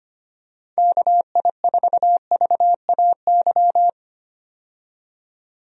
Morse code, heard as a keyed tone, about 25 wpm, 700 Hz. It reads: KI4VAY
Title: USA Call Signs - 6 Characters - 25wpm